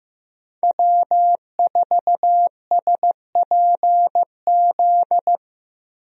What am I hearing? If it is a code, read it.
Morse code, heard as a keyed tone, about 15 wpm, 700 Hz. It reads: W4SPZ
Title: USA Call Signs - 5 Characters - 15wpm